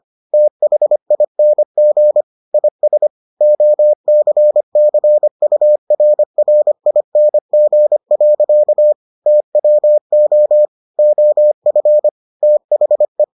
Morse code, heard as a keyed tone, about 25 wpm, 600 Hz. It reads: THING IS OCCURRING. TWO OF THE